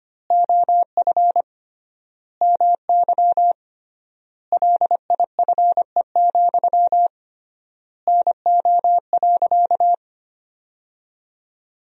Morse code, heard as a keyed tone, about 25 wpm, 700 Hz. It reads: OF MY LIFE, NO.
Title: 3x Word-Spacing - Sentences from Top 300 Words - 25wpm